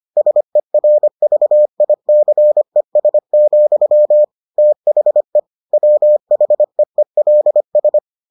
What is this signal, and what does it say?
Morse code, heard as a keyed tone, about 25 wpm, 600 Hz. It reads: SERVICES, THE WHEELS